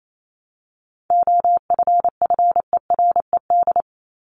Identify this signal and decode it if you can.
Morse code, heard as a keyed tone, about 28 wpm, 700 Hz. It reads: OFFERED
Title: ED Suffix - Top 100 Words with Base Words 28wpm